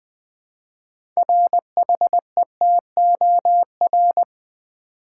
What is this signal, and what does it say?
Morse code, heard as a keyed tone, about 20 wpm, 700 Hz. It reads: RHETOR
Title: Build A Word - 3 Letters At A Time - 8-Letter Words - 20wpm